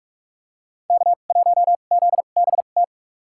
Morse code, heard as a keyed tone, about 42 wpm, spaced as for 31 wpm, 700 Hz. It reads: K1ZBT
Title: USA Call Signs - 2x Character Spacing - 45wpm